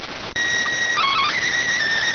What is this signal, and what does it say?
SSTV (slow-scan television), PD 290.